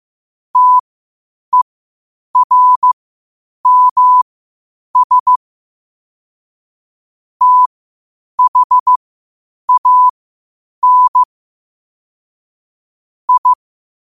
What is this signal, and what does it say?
Morse code, heard as a keyed tone, about 15 wpm, spaced as for 8 wpm, 1000 Hz. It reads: TERMS THAN I